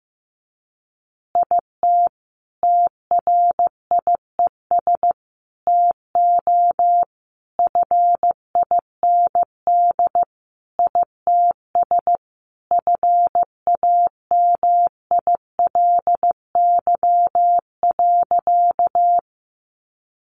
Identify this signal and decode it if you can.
Morse code, heard as a keyed tone, about 15 wpm, 700 Hz. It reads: IT TRIES TO FIND ITS FAMILY.
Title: Additional Easy Phrases 15wpm